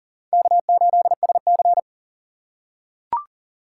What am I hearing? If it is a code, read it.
Morse code, heard as a keyed tone, about 40 wpm, 700 Hz. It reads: K8SC
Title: USA Call Signs - 4 Characters - 2x Character Spacing - 40wpm